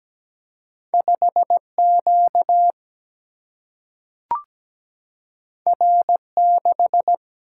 Morse code, heard as a keyed tone, about 17 wpm, 700 Hz. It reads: 5Q R6
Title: Single-Letter Single-Number 17wpm